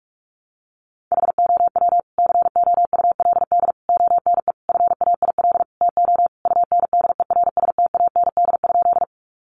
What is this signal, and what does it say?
Morse code, heard as a keyed tone, about 44 wpm, 700 Hz. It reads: HOW COULD ONE FAIL TO UNDERSTAND?
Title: Sentences from Top 1100 Words 45wpm